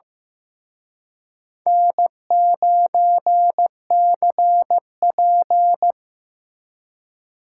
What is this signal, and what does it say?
Morse code, heard as a keyed tone, about 18 wpm, 700 Hz. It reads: N9CP
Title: Speed Racing - USA 1x2 Call Signs 15wpm